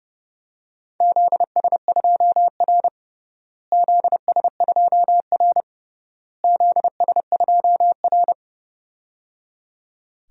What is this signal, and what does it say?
Morse code, heard as a keyed tone, about 30 wpm, 700 Hz. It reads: ZS2R ZS2R ZS2R
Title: DXCC-100 Call Signs - 4 Characters - Repeated 3x - 30wpm